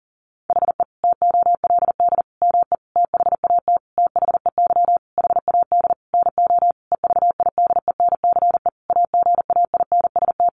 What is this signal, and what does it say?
Morse code, heard as a keyed tone, about 40 wpm, 700 Hz. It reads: HE TOLD ME THAT THEY HAD NO EVIDENCE AGAINST
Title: Sentences from Top 1300 Words 40wpm